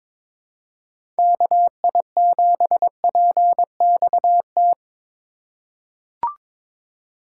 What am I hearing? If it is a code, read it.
Morse code, heard as a keyed tone, about 22 wpm, 700 Hz. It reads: KI7PXT
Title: USA Call Signs - 6 Characters - 3x Character Spacing - 22wpm